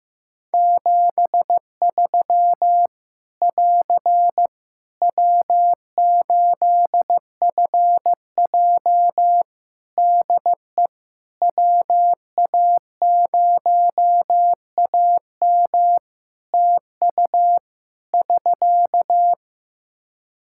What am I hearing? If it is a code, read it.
Morse code, heard as a keyed tone, about 14 wpm, 700 Hz. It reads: 73 <AR> W8FJ DE WA0AM TU <SK>
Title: Practice QSO - Exchange 4 - Ending - 15wpm